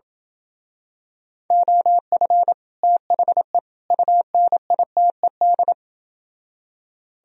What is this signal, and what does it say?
Morse code, heard as a keyed tone, about 28 wpm, 700 Hz. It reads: OF THE UNITED